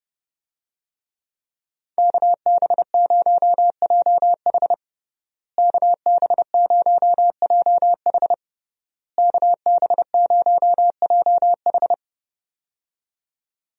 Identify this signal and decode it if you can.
Morse code, heard as a keyed tone, about 31 wpm, 700 Hz. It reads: KB0JH KB0JH KB0JH
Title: USA Call Signs - 5 Characters - Repeated 3x - 30wpm